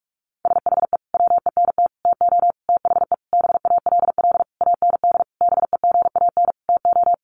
Morse code, heard as a keyed tone, about 44 wpm, 700 Hz. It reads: SHE WENT TO THE BALL AND BEGAN TO
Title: Sentences from Top 900 Words 45wpm